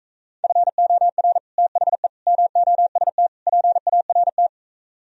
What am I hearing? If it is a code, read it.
Morse code, heard as a keyed tone, about 45 wpm, 690 Hz. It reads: FOR THE MOST PART